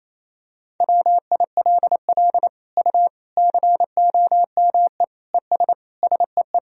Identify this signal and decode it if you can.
Morse code, heard as a keyed tone, about 28 wpm, 700 Hz. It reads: WILL U COME ES SEE
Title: Sentences from Top 100 Words and Abbreviations 28wpm